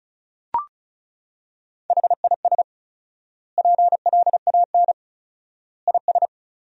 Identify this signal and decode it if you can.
Morse code, heard as a keyed tone, about 36 wpm, 700 Hz. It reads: HIS PLAN IS
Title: 4x Word-Spacing - Sentences from Top 500 Words - 35wpm